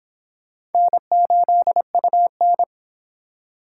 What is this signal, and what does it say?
Morse code, heard as a keyed tone, about 29 wpm, 700 Hz. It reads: N8UN